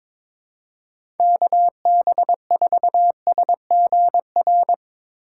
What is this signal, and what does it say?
Morse code, heard as a keyed tone, about 23 wpm, 700 Hz. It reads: KB4SGR
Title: USA Call Signs - 6 Characters - 6x Character Spacing - 22wpm